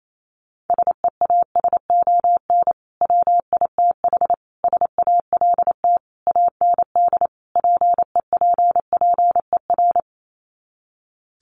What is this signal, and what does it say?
Morse code, heard as a keyed tone, about 28 wpm, 700 Hz. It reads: SEASON WITH SALT AND PEPPER
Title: Sets of 5 Words 28wpm